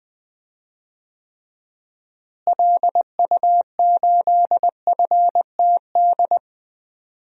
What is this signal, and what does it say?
Morse code, heard as a keyed tone, about 20 wpm, 700 Hz. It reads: LU8FTD
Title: DXCC-100 Call Signs - 6 Characters - Repeated 2x - 20wpm